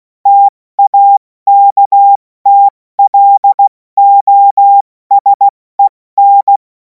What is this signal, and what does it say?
Morse code, heard as a keyed tone, about 16 wpm, 800 Hz. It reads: TAKTLOSEN
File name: Morsekurs / Witze / Witz 03. Februar 2026 70 F80